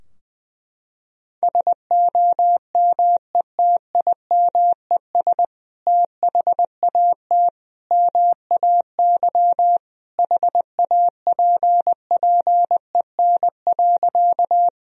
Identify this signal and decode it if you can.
Morse code, heard as a keyed tone, about 20 wpm, 700 Hz. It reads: SOMETIMES THAT MAY HAPPEN.
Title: Sentences from Top 1000 Words - Review Part 3 - 20wpm